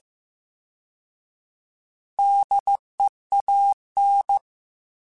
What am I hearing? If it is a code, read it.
Morse code, heard as a keyed tone, about 15 wpm, 800 Hz. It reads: DEAN
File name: Alphabetical / Alphabetical mp3s / Ds @13